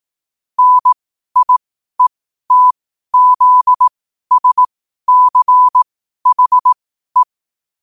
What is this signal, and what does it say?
Morse code, heard as a keyed tone, about 18 wpm, spaced as for 12 wpm, 1000 Hz. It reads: NIETZSCHE
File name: qotdfiles / 12wpm